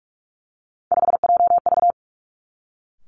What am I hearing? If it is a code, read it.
Morse code, heard as a keyed tone, about 46 wpm, 700 Hz. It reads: 5JV
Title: Three Challenging Letter-Number Combinations 45wpm